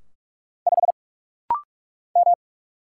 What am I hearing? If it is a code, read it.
Morse code, heard as a keyed tone, about 44 wpm, 700 Hz. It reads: 5 M
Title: Single Letter-Number Alt - Rapid-Fire 45wpm